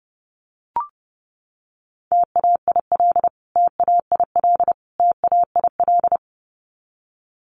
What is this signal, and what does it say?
Morse code, heard as a keyed tone, about 30 wpm, 700 Hz. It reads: TAIL TAIL TAIL
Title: Top 500 Words Repeated 30wpm